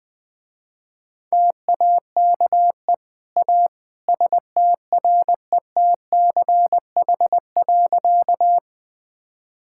Morse code, heard as a keyed tone, about 20 wpm, 700 Hz. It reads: TAKE A STRETCH.
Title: Easy Sentences with Hints - 20wpm